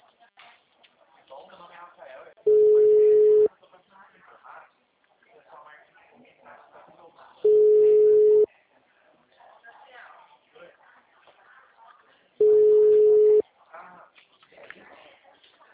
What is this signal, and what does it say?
Ringback tone (Europe, 425 Hz), with speech.